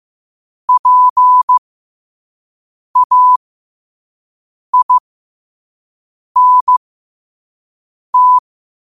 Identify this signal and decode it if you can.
Morse code, heard as a keyed tone, about 15 wpm, spaced as for 5 wpm, 1000 Hz. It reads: PAINT
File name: qotdfiles / 5wpm